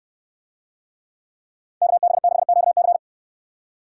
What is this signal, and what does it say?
Morse code, heard as a keyed tone, about 62 wpm, 700 Hz. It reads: KB6QC